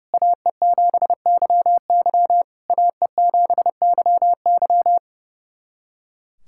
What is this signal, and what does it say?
Morse code, heard as a keyed tone, about 30 wpm, 700 Hz. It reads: AE7YY AE7YY
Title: SOTA QSO Practice 30wpm